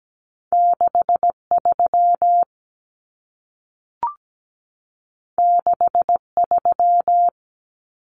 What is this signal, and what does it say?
Morse code, heard as a keyed tone, about 17 wpm, 700 Hz. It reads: 63 63
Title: Numbers - 2-Digit 17wpm